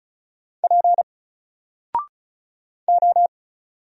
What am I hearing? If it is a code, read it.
Morse code, heard as a keyed tone, about 36 wpm, 700 Hz. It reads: P O
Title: Single Character (Letter:Number:Punctuation:Prosign) Alt 35wpm